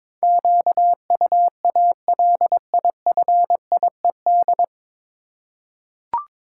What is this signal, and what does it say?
Morse code, heard as a keyed tone, about 22 wpm, 700 Hz. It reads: QUALIFIED
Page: Morse Code Ninja practice 22wpm